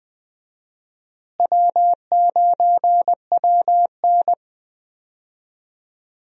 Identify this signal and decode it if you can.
Morse code, heard as a keyed tone, about 20 wpm, 700 Hz. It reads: W9WN
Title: USA Call Signs - 4 Characters - Repeated 3x - 20wpm